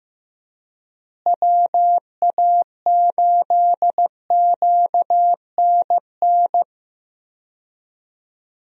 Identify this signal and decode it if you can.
Morse code, heard as a keyed tone, about 15 wpm, 700 Hz. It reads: WA8QNN
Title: USA Call Signs - 6 Characters - Repeated 3x - 15wpm